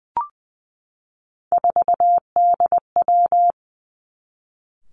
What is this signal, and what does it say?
Morse code, heard as a keyed tone, about 19 wpm, 700 Hz. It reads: 4DW
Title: Three Challenging Letter-Number Combinations 20wpm